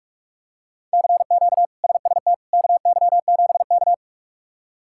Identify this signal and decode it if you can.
Morse code, heard as a keyed tone, about 41 wpm, 700 Hz. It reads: CQ SST KY7K
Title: K1USN SST QSO Practice - 45wpm